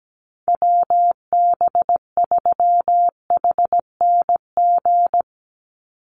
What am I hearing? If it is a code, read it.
Morse code, heard as a keyed tone, about 17 wpm, 700 Hz. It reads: WB3HNG
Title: USA Call Signs 17wpm